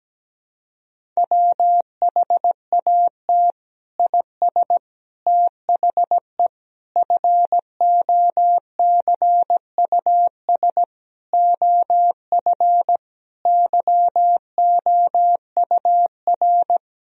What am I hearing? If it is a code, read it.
Morse code, heard as a keyed tone, about 17 wpm, 700 Hz. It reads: WHAT IS THE FOCUS OF YOUR
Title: Sentences from Top 1000 Words - Review Part 4 - 17wpm